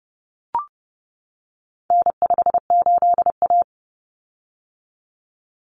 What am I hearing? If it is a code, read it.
Morse code, heard as a keyed tone, about 30 wpm, 700 Hz. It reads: N58A
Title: Common Cut Numbers - 4-Digit 30wpm